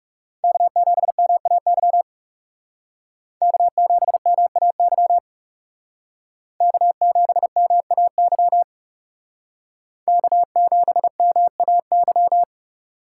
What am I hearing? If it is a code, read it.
Morse code, heard as a keyed tone, about 38 wpm, 700 Hz. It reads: K7MAY K7MAY K7MAY K7MAY